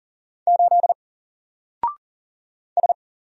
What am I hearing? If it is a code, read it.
Morse code, heard as a keyed tone, about 39 wpm, 700 Hz. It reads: 8 S
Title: Single Letter-Number Alt 40wpm